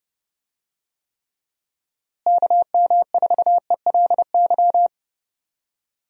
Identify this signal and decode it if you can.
Morse code, heard as a keyed tone, about 30 wpm, 700 Hz. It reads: KM4ELY
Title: USA Call Signs - 6 Characters - Repeated 2x - 30wpm